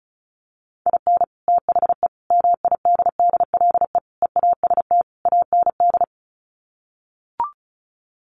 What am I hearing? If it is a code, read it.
Morse code, heard as a keyed tone, about 34 wpm, 700 Hz. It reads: IN THE MIDDLE EAST AND
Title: Sets of 5 Words Encore 35wpm